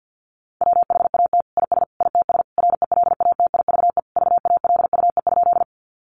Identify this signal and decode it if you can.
Morse code, heard as a keyed tone, about 50 wpm, 700 Hz. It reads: WHAT IS ITS RELATIVE VALUE?
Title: Sentences from Top 1400 Words 50wpm